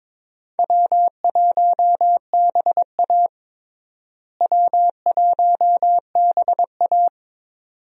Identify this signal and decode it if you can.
Morse code, heard as a keyed tone, about 22 wpm, 700 Hz. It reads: W1BA W1BA
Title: USA Call Signs - 4 Characters - Repeated 2x - 22wpm